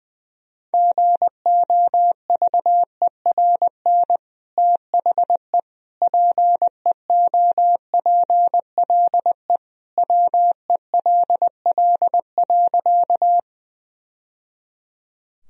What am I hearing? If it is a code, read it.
Morse code, heard as a keyed tone, about 20 wpm, 700 Hz. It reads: GOVERN THE PEOPLE WELL.
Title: Sentences from Top 1000 Words - Review Part 2 - 20wpm